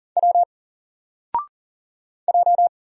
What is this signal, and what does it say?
Morse code, heard as a keyed tone, about 40 wpm, 700 Hz. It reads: W J
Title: Single Letter-Number Alt 40wpm